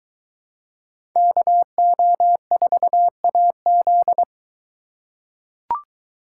Morse code, heard as a keyed tone, about 21 wpm, 700 Hz. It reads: KO4AZ